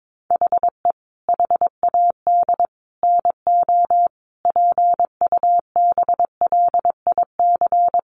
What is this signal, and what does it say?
Morse code, heard as a keyed tone, about 22 wpm, 700 Hz. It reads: HE HAD NO PUBLIC
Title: Sentences from Top 1500 Words 22wpm